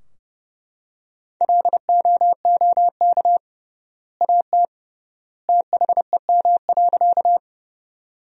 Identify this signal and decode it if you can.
Morse code, heard as a keyed tone, about 30 wpm, 700 Hz. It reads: LOOK AT THEM.
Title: 3x Word-Spacing - Sentences from Top 100 Words - 30wpm